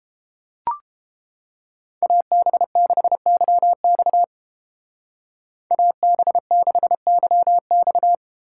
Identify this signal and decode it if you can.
Morse code, heard as a keyed tone, about 30 wpm, 700 Hz. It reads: AB6YX AB6YX